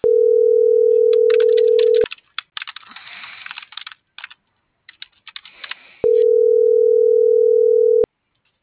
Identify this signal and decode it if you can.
Ringback tone (North America), 440 + 480 Hz.